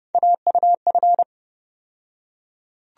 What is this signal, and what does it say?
Morse code, heard as a keyed tone, about 29 wpm, 700 Hz. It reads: AUF
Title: USA Call Sign Postfixes - 3 Letters 30wpm